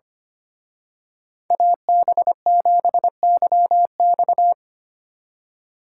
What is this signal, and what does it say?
Morse code, heard as a keyed tone, about 29 wpm, 700 Hz. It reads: AB7YX
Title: Speed Racing - USA 2x2 Call Signs 25wpm